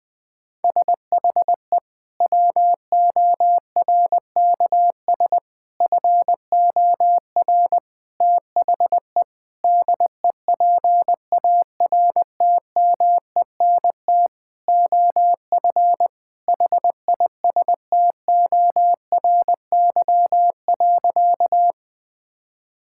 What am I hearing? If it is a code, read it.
Morse code, heard as a keyed tone, about 20 wpm, 700 Hz. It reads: SHE WORKS FOR THE DEPARTMENT OF HISTORY.
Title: Sentences from Top 1000 Words - Review Part 5 - 20wpm